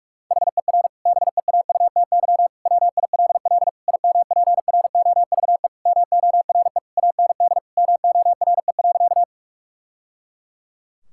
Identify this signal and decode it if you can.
Morse code, heard as a keyed tone, about 43 wpm, 700 Hz. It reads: HER BEAUTY WILL IMPROVE MORE AND MORE.
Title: Sentences from Top 1000 Words - Review Part 5 - 45wpm